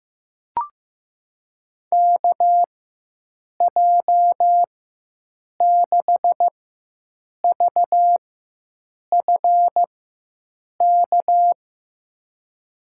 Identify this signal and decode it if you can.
Morse code, heard as a keyed tone, about 15 wpm, spaced as for 7 wpm, 700 Hz. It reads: KJ6VFK
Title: USA Call Signs - 6 Characters - 4x Character Spacing - 15wpm